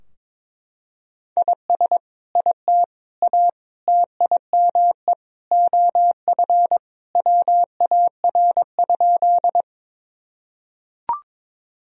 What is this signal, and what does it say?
Morse code, heard as a keyed tone, about 22 wpm, 700 Hz. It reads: IS IT A TIME OF WAR?
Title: Sentences from Top 1000 Words - Review Part 2 - 22wpm